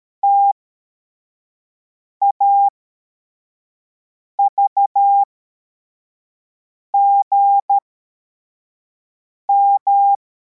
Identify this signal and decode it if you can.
Morse code, heard as a keyed tone, about 13 wpm, spaced as for 4 wpm, 800 Hz. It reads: TAVGM